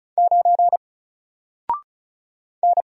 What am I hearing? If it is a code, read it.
Morse code, heard as a keyed tone, about 36 wpm, 700 Hz. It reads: 9 N
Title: Single Letter-Number Alt 35wpm